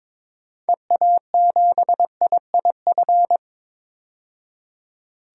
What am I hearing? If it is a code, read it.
Morse code, heard as a keyed tone, about 22 wpm, 700 Hz. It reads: EA7IIF
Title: DXCC-100 Call Signs - 6 Characters - 22wpm